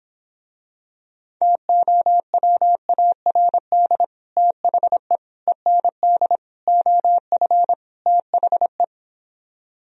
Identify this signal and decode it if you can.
Morse code, heard as a keyed tone, about 26 wpm, 700 Hz. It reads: TOWARD THE END OF THE